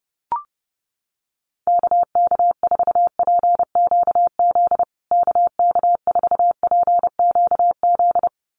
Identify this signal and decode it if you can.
Morse code, heard as a keyed tone, about 30 wpm, 700 Hz. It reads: KK4PQZ KK4PQZ
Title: USA Call Signs Repeated 2x 30wpm